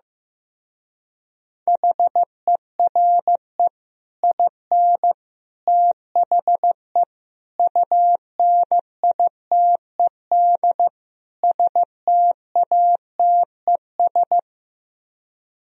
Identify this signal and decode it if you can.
Morse code, heard as a keyed tone, about 15 wpm, 700 Hz. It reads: HERE IN THE UNITED STATES
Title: Speed-Racing - Sets of 5 Words 15wpm